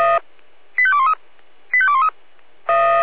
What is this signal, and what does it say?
Five-tone selective call (ZVEI), heard as quick runs of five tones, tones of about 80 ms. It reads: 85213 85213